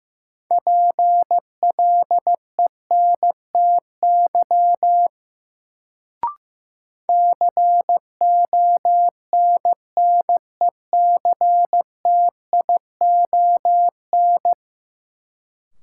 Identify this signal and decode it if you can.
Morse code, heard as a keyed tone, about 15 wpm, 700 Hz. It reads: PLENTY CONNECTION
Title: Sentences from Top 1300 Words - New Word Drill - 15wpm